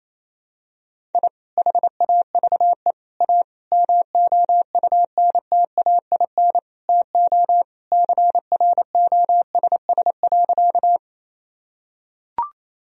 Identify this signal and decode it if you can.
Morse code, heard as a keyed tone, about 28 wpm, 700 Hz. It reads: I HAVE A MOUNTAIN TO CROSS.
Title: Sentences from Top 1000 Words - Review Part 1 - 28wpm